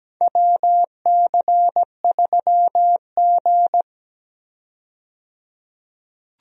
Morse code, heard as a keyed tone, about 17 wpm, 700 Hz. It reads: WC3G
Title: USA Call Signs - 4 Characters - 17wpm